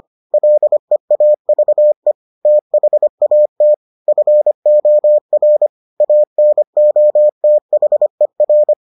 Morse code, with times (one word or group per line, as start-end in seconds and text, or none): 0.3-2.1 LEAVE
2.5-3.7 THAT
4.1-5.7 FOR
6.0-8.7 ANOTHER